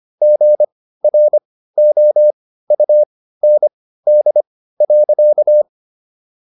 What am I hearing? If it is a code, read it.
Morse code, heard as a keyed tone, about 25 wpm, spaced as for 15 wpm, 600 Hz. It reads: GROUND.